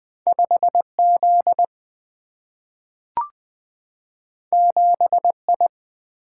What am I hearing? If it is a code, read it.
Morse code, heard as a keyed tone, about 20 wpm, 700 Hz. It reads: 5Z 7I
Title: Single-Letter Single-Number 20wpm